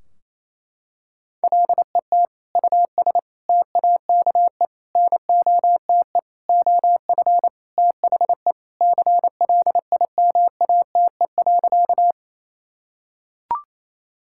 Morse code, heard as a keyed tone, about 27 wpm, 700 Hz. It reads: LET US TAKE NOTE OF THE CLIMATE.
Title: Sentences from Top 1600 Words 28wpm